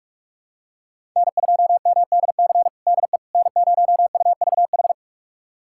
Morse code, heard as a keyed tone, about 43 wpm, 700 Hz. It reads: N2MDC DE N0UVH
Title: USA Callsign DE Callsign - 5 Characters - 45wpm